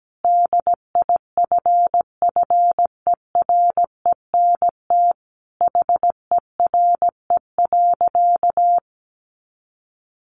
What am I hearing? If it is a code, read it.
Morse code, heard as a keyed tone, about 17 wpm, 700 Hz. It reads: DIFFERENT HERE.
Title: Easy Phrases 17wpm